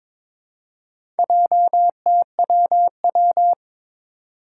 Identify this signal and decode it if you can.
Morse code, heard as a keyed tone, about 22 wpm, 700 Hz. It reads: JTWW
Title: Four Challenging Letter-Number Combinations 22wpm